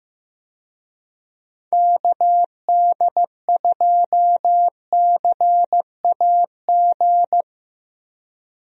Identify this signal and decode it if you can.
Morse code, heard as a keyed tone, about 15 wpm, 700 Hz. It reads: KD2CAG
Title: USA Call Signs - 6 Characters - 15wpm